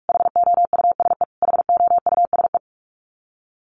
Morse code, heard as a keyed tone, about 44 wpm, 700 Hz. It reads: HOUSE HOUSE
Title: Top 500 Words Repeated 45wpm